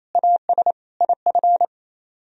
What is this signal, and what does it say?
Morse code, heard as a keyed tone, about 34 wpm, 700 Hz. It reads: AS IF
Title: Speed-Racing - Sets of 4 Words 28wpm